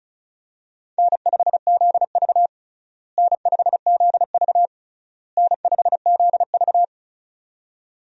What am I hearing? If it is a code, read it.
Morse code, heard as a keyed tone, about 36 wpm, 700 Hz. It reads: N5ZV N5ZV N5ZV
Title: USA Call Signs - 4 Characters - Repeated 3x - 35wpm